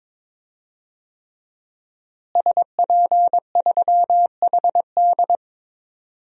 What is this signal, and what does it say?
Morse code, heard as a keyed tone, about 21 wpm, 700 Hz. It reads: SP3HD
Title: DXCC-100 Call Signs - 5 Characters - Repeated 3x - 22wpm